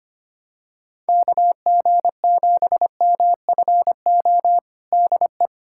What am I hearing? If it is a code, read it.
Morse code, heard as a keyed tone, about 25 wpm, 700 Hz. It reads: KG7MFO DE
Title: USA Callsign DE Callsign 25wpm